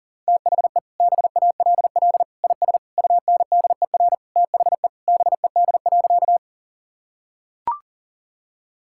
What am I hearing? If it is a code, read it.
Morse code, heard as a keyed tone, about 37 wpm, 700 Hz. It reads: THE BALL IS UNDER THE BED.
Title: Sentences from Top 1000 Words - Review Part 6 - 40wpm